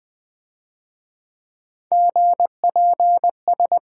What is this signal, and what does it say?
Morse code, heard as a keyed tone, about 22 wpm, 700 Hz. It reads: GPS
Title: Difficult Words - Speed-Racing with Hints - 20wpm